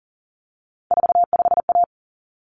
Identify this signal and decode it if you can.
Morse code, heard as a keyed tone, about 40 wpm, 700 Hz. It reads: 45A
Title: Common Cut Numbers - 3-Digit 40wpm